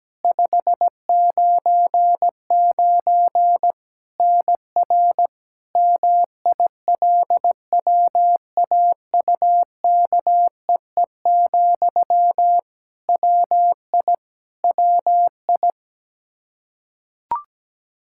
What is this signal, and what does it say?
Morse code, heard as a keyed tone, about 17 wpm, 700 Hz. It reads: 599 NR MILWAUKEE, WI WI
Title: Practice QSO - Exchange 1 - Beginning - 17wpm